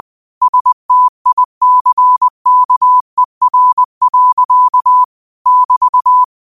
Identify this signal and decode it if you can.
Morse code, heard as a keyed tone, about 20 wpm, 1000 Hz. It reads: STICKER. <BT>